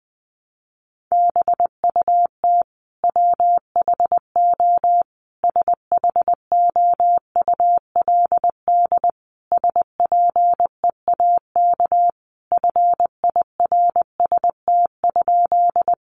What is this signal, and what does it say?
Morse code, heard as a keyed tone, about 20 wpm, 700 Hz. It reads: BUT WHO SHOULD SPEAK FIRST?
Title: Sentences from Top 1100 Words 20wpm